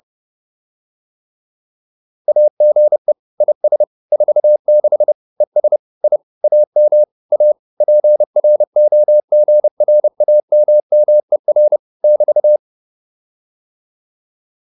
Morse code, heard as a keyed tone, about 30 wpm, 600 Hz. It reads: AGE IS 46 ES I AM A PROGRAMMER <BT>
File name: QSO302 30